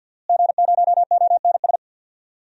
Morse code, heard as a keyed tone, about 52 wpm, 700 Hz. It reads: G0ONS
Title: DXCC-100 Call Signs - 5 Characters - Repeated 3x - 50wpm